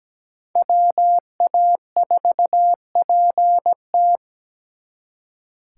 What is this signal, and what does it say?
Morse code, heard as a keyed tone, about 20 wpm, 700 Hz. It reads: WA4PT
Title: Speed Racing - USA 2x2 Call Signs 17wpm